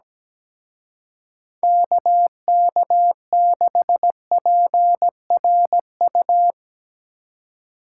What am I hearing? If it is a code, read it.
Morse code, heard as a keyed tone, about 19 wpm, 700 Hz. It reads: KK6PRU
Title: Speed Racing - USA 2x3 Call Signs 17wpm